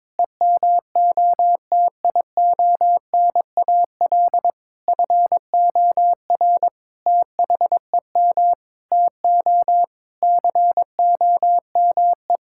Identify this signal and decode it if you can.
Morse code, heard as a keyed tone, about 22 wpm, 700 Hz. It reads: EMOTIONAL FOR THEM TO COME
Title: Sentences from Top 1000 Words - Review Part 5 - 22wpm